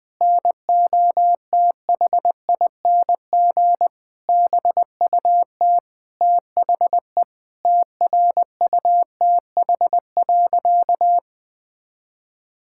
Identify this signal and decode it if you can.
Morse code, heard as a keyed tone, about 20 wpm, 700 Hz. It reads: NOTHING BUT THE TRUTH.
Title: Sentences with 8-Word Phrases - 20wpm